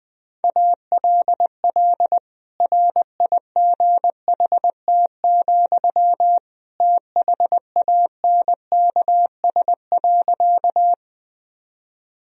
Morse code, heard as a keyed tone, about 20 wpm, 700 Hz. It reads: ALL RIGHT, THANKS.
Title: Sentences from Top 1000 Words - Review Part 3 - 20wpm